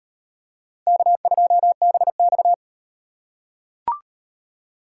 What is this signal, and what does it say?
Morse code, heard as a keyed tone, about 33 wpm, 700 Hz. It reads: K2BX